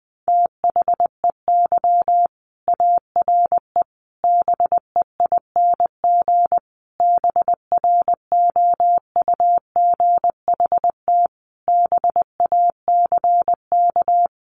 Morse code, heard as a keyed tone, about 20 wpm, 700 Hz. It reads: THEY ARE BEING BROUGHT BACK
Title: Sentences from Top 600 Words 20wpm